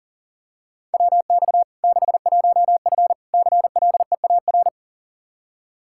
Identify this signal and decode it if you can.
Morse code, heard as a keyed tone, about 40 wpm, 700 Hz. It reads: WX 61F CLEAR
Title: K1USN SST QSO Practice 40wpm